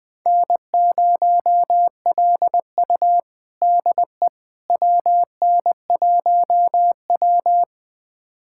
Morse code, heard as a keyed tone, about 20 wpm, 700 Hz. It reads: N0LU DE WN1W
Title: USA Callsign DE Callsign - 4 Characters - 20wpm